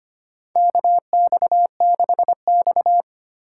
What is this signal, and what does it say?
Morse code, heard as a keyed tone, about 25 wpm, 700 Hz. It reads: KX6X
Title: K1USN SST QSO Practice - 25wpm